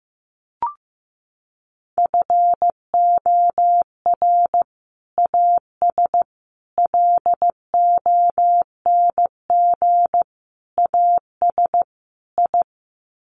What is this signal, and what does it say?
Morse code, heard as a keyed tone, about 15 wpm, 700 Hz. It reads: FOR AS LONG AS I
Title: Sets of 5 Words Encore2 15wpm